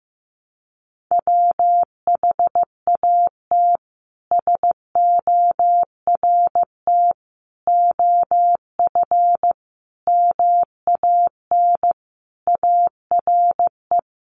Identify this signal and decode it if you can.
Morse code, heard as a keyed tone, about 15 wpm, 700 Hz. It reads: WHAT SORT OF MAN ARE
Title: Sentences from Top 1300 Words 15wpm